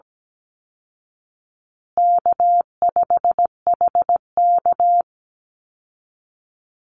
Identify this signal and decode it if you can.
Morse code, heard as a keyed tone, about 17 wpm, 700 Hz. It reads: K5HK
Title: Easy Call Signs Repeated 2x - 17wpm